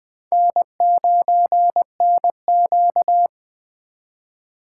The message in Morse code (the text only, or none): N9NQ